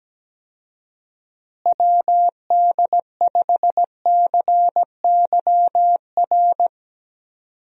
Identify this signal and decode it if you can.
Morse code, heard as a keyed tone, about 17 wpm, 700 Hz. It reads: WD5CYR
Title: USA Call Signs - 6 Characters - Repeated 3x - 17wpm